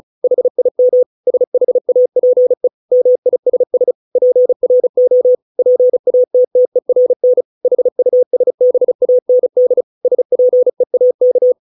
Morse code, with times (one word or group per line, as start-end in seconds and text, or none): 0.2-1.0 HIM
1.3-2.7 SHAPE
2.9-3.9 MISS
4.2-5.4 PRO
5.6-7.4 PATTERN
7.7-9.8 HUSBAND
10.1-11.5 SPEAK